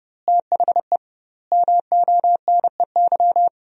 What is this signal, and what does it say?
Morse code, heard as a keyed tone, about 30 wpm, 700 Hz. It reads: THE MONEY
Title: 2x Word-Spacing - Sentences from Top 400 Words - 30wpm